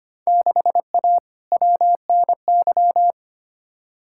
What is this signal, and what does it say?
Morse code, heard as a keyed tone, about 25 wpm, 700 Hz. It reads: 6A WNY
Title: Field Day Exchanges - 25wpm